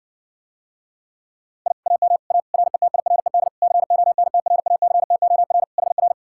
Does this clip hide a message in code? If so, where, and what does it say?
Yes: Morse code from 1.7 to 6.1 s, heard as a keyed tone, about 60 wpm, 700 Hz. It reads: I AM A RETIRED CONTRACTOR HR